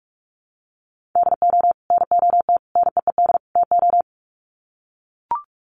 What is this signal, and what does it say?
Morse code, heard as a keyed tone, about 44 wpm, 700 Hz. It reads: DO NOT NEED TO
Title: Sets of 4 Words 45wpm